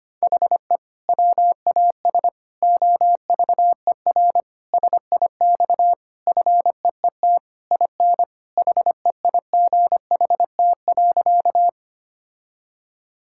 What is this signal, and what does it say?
Morse code, heard as a keyed tone, about 25 wpm, 700 Hz. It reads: HE WAS OVER SIX FEET IN HEIGHT.
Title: Sentences from Top 1600 Words 25wpm